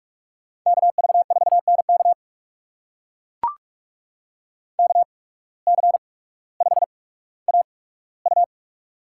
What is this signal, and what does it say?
Morse code, heard as a keyed tone, about 44 wpm, spaced as for 19 wpm, 690 Hz. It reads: KV4NK KC5AU
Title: USA Call Signs - 5 Characters - 8x Character Spacing - 45wpm